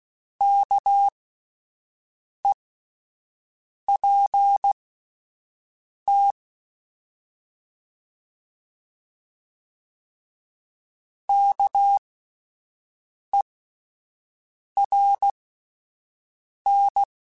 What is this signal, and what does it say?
Morse code, heard as a keyed tone, about 16 wpm, spaced as for 5 wpm, 800 Hz. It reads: KEPT KERN